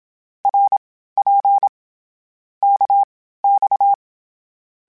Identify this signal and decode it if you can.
Morse code, heard as a keyed tone, about 27 wpm, spaced as for 15 wpm, 800 Hz. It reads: RP KX
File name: Short Runs / Short Runs-mp3s / Balanced @25